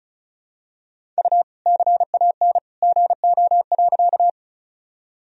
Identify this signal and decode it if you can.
Morse code, heard as a keyed tone, about 36 wpm, 700 Hz. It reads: U CAN GO.
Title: Sentences from Top 100 Words and Abbreviations 35wpm